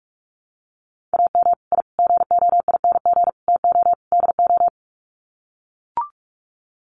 Morse code, heard as a keyed tone, about 44 wpm, 700 Hz. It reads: AM I GOING TO DO
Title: Sets of 5 Words Encore 45wpm